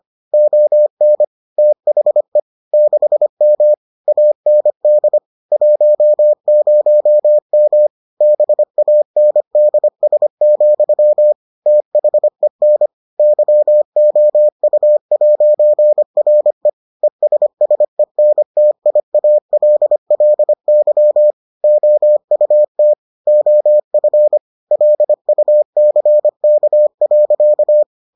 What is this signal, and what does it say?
Morse code, heard as a keyed tone, about 25 wpm, 600 Hz. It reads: ON THE 6M AND 10M BANDS, THEN YOU'RE ESSENTIALLY OUT OF LUCK.